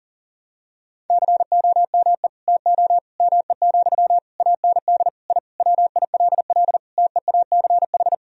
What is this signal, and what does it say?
Morse code, heard as a keyed tone, about 40 wpm, 700 Hz. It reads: COME TO ME, AND I WILL TEACH
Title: Sentences from Top 1000 Words - Review Part 2 - 40wpm